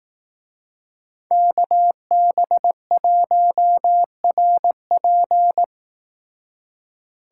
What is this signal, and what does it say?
Morse code, heard as a keyed tone, about 18 wpm, 700 Hz. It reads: KB1RP
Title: Speed Racing - USA 2x2 Call Signs 15wpm